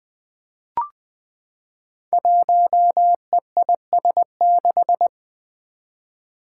Morse code, heard as a keyed tone, about 20 wpm, 700 Hz. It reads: 1EIS6
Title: Five Challenging Letter-Number Combinations 20wpm